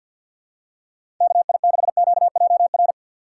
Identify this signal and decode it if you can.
Morse code, heard as a keyed tone, about 51 wpm, 700 Hz. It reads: KI6QJR
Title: USA Call Signs - 6 Characters - 4x Character Spacing - 50wpm